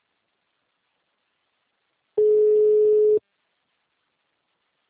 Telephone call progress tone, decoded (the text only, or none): ringback tone